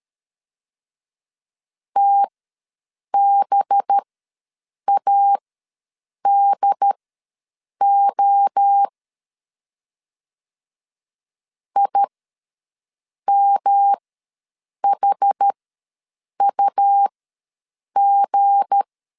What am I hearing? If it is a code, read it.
Morse code, heard as a keyed tone, about 13 wpm, spaced as for 7 wpm, 800 Hz. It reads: TBADO IMHUG
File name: LEZIONI / 9 2 ws 12wpm 3406